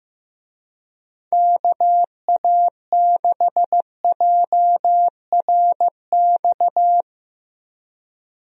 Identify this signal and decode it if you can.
Morse code, heard as a keyed tone, about 15 wpm, 700 Hz. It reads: KA6JRX
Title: USA Call Signs - 6 Characters - Repeated 3x - 15wpm